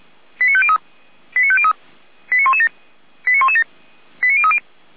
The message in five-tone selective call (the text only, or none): 89503 89503 89117 89117 89030